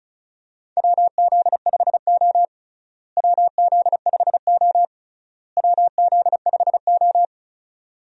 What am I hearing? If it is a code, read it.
Morse code, heard as a keyed tone, about 36 wpm, 700 Hz. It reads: WZ5O WZ5O WZ5O
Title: USA Call Signs - 4 Characters - Repeated 3x - 35wpm